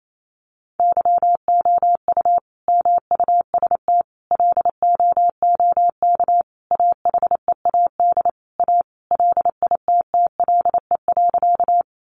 Morse code, heard as a keyed tone, about 28 wpm, 700 Hz. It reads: YOU MUST LOOK AHEAD A LITTLE.
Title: Sentences from Top 1100 Words 28wpm